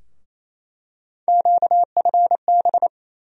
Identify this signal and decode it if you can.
Morse code, heard as a keyed tone, about 27 wpm, 700 Hz. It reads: QFB
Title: USA Call Sign Postfixes - 3 Letters 28wpm